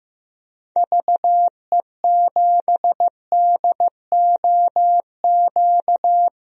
Morse code, heard as a keyed tone, about 15 wpm, 700 Hz. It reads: VE7DOQ
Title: DXCC-100 Call Signs - 6 Characters - 15wpm